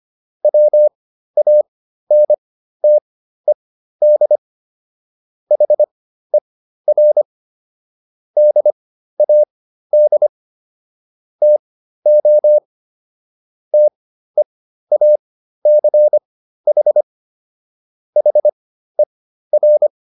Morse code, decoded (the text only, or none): WANTED HER DAD TO TEACH HER